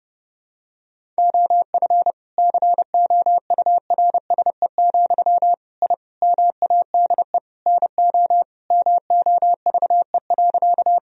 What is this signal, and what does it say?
Morse code, heard as a keyed tone, about 30 wpm, 700 Hz. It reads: OF COURSE, I MADE NO MOVE.
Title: Sentences from Top 1000 Words - Review Part 1 - 30wpm